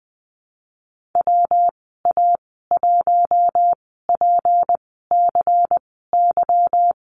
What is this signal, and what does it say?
Morse code, heard as a keyed tone, about 20 wpm, spaced as for 15 wpm, 700 Hz. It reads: WA1PCY
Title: 167 Lesson 27g - Number 7 - Call Signs 20wpm